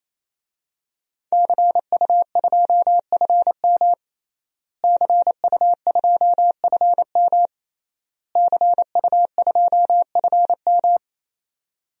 Morse code, heard as a keyed tone, about 28 wpm, 700 Hz. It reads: CU2FM CU2FM CU2FM
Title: DXCC-100 Call Signs - 5 Characters - Repeated 3x - 28wpm